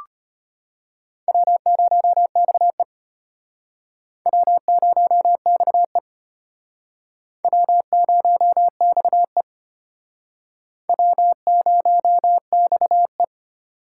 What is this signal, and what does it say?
Morse code, heard as a keyed tone, about 32 wpm, 700 Hz. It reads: W0XE W0XE W0XE W0XE